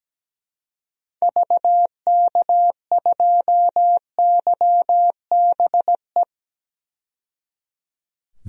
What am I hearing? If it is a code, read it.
Morse code, heard as a keyed tone, about 17 wpm, 700 Hz. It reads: VK2YBE
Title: DXCC-100 Call Signs - 6 Characters - 17wpm